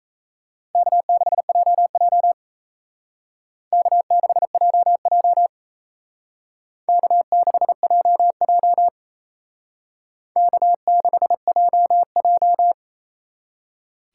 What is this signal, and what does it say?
Morse code, heard as a keyed tone, about 35 wpm, 700 Hz. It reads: K6JJ K6JJ K6JJ K6JJ